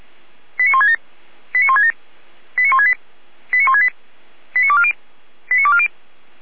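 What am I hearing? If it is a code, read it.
Five-tone selective call (ZVEI), heard as quick runs of five tones, tones of about 70 ms. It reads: 89167 89167 89168 89168 89240 89240
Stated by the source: ZVEI five-tone selective call received by SDR, fire brigade Feuerwehr Orlamünde.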